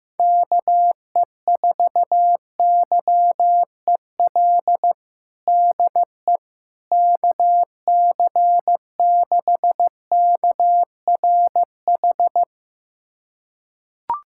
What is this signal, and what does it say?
Morse code, heard as a keyed tone, about 15 wpm, 700 Hz. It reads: KE4YEL DE KC6KRH
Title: USA Callsign DE Callsign 15wpm